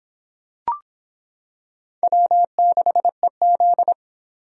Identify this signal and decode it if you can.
Morse code, heard as a keyed tone, about 26 wpm, 700 Hz. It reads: W6EZ